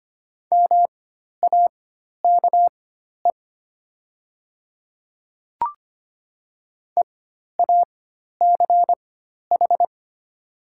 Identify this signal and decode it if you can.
Morse code, heard as a keyed tone, about 25 wpm, spaced as for 12 wpm, 690 Hz. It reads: MAKE EACH
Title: 4x Character Spacing - Top 300 Words - 25wpm